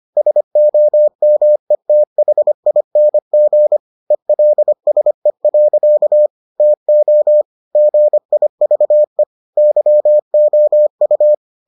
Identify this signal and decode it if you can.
Morse code, heard as a keyed tone, about 25 wpm, 600 Hz. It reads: SOMETHING ELSE. TO GIVE YOU